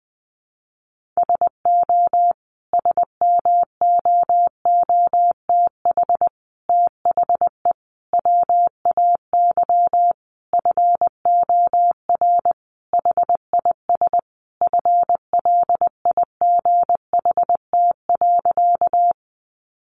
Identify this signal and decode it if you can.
Morse code, heard as a keyed tone, about 20 wpm, 700 Hz. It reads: SO SMOOTH THE WAY FOR HIS FLIGHT.
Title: Sentences from Top 1400 Words 20wpm